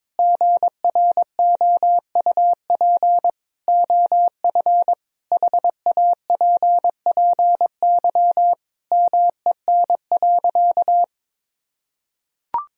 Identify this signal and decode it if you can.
Morse code, heard as a keyed tone, about 22 wpm, 700 Hz. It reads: GROUP OF HAPPY MEN.
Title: Sentences from Top 1000 Words - Review Part 1 - 22wpm